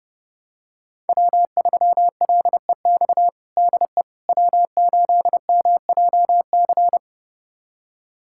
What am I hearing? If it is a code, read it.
Morse code, heard as a keyed tone, about 32 wpm, 700 Hz. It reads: W3LEX DE W8MJC
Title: USA Callsign DE Callsign - 5 Characters - 30wpm